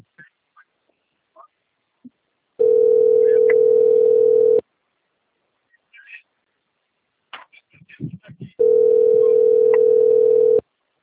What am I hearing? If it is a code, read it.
Ringback tone (North America), 440 + 480 Hz.
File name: out- - - - - .47841